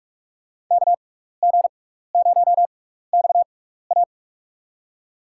Morse code, heard as a keyed tone, about 42 wpm, spaced as for 15 wpm, 700 Hz. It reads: KG0XA
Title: USA Call Signs - 6x Character Spacing - 45wpm